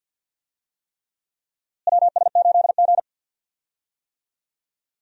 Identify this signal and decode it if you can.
Morse code, heard as a keyed tone, about 52 wpm, 700 Hz. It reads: WS8G
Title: USA Call Signs - 4 Characters - Repeated 3x - 50wpm